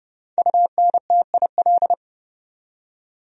Morse code, heard as a keyed tone, about 37 wpm, 700 Hz. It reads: UNTIL
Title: Speed-Racing Top 500 Words 30wpm